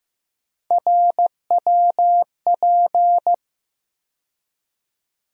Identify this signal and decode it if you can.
Morse code, heard as a keyed tone, about 15 wpm, 700 Hz. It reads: RWP
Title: USA Call Sign Postfixes - 3 Letters 15wpm